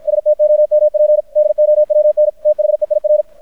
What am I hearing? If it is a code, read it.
Morse code, heard as a keyed tone, about 53 wpm, 600 Hz. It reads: STOMP DOWN THEIR
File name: archived / CRSnet